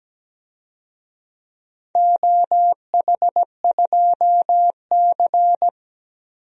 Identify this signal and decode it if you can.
Morse code, heard as a keyed tone, about 17 wpm, 700 Hz. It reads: OH2C
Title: DXCC-100 Call Signs - 4 Characters - 17wpm